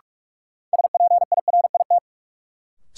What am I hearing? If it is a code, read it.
Morse code, heard as a keyed tone, about 43 wpm, 700 Hz. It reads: SPIRIT
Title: Sentences from Top 1400 Words - New Word Drill Alt - 45wpm